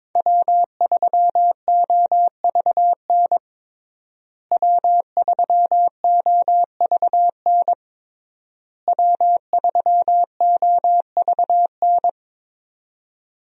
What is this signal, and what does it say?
Morse code, heard as a keyed tone, about 22 wpm, 700 Hz. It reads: W3OVN W3OVN W3OVN
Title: USA Call Signs - 5 Characters - Repeated 3x - 22wpm